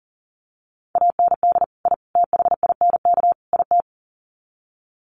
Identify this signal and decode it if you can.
Morse code, heard as a keyed tone, about 40 wpm, 700 Hz. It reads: AND I THINK IT
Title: Sets of 4 Words 40wpm